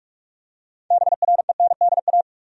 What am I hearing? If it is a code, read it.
Morse code, heard as a keyed tone, about 46 wpm, 700 Hz. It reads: BRENDA
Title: USA Common Names - 45wpm